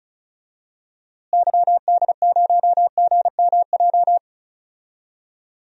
Morse code, heard as a keyed tone, about 36 wpm, 700 Hz. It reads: YD0GMJ
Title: DXCC-100 Call Signs - 6 Characters - Repeated 2x - 35wpm